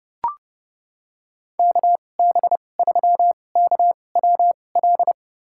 Morse code, heard as a keyed tone, about 30 wpm, spaced as for 22 wpm, 700 Hz. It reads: KB3KWL
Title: USA Call Signs - 2x Character Spacing - 30wpm